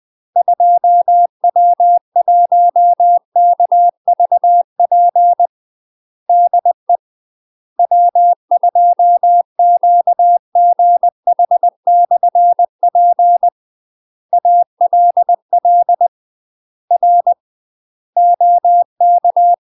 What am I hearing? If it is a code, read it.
Morse code, heard as a keyed tone, about 20 wpm, 700 Hz. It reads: 2W1KVP DE W2QGH/P ALL R OK
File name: UK MASTER QSOs TRAINING / 20WPM / UK MASTER QSOs 0053